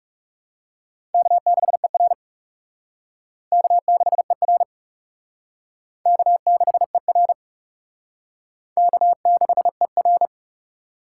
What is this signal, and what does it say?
Morse code, heard as a keyed tone, about 36 wpm, 700 Hz. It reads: K6ER K6ER K6ER K6ER
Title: Speed Racing - USA Call Signs 30wpm